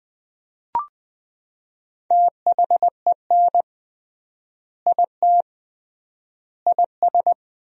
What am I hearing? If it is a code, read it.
Morse code, heard as a keyed tone, about 20 wpm, 700 Hz. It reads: THEN IT IS
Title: 3x Word-Spacing - Sentences from Top 500 Words - 20wpm